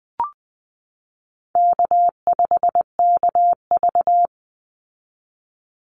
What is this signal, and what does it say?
Morse code, heard as a keyed tone, about 20 wpm, 700 Hz. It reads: K5KV
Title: Easy Call Signs Repeated 2x - 20wpm